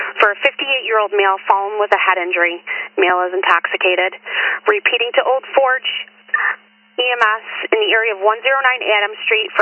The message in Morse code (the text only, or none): none